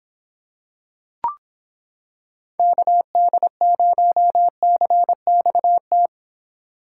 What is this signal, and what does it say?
Morse code, heard as a keyed tone, about 26 wpm, 700 Hz. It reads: KD0CXT